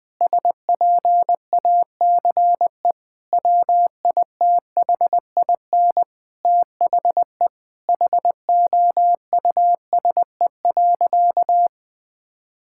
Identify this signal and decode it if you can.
Morse code, heard as a keyed tone, about 21 wpm, 700 Hz. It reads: SPACE WITHIN THE HOUSE.
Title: Sentences from Top 1000 Words - Review Part 5 - 20wpm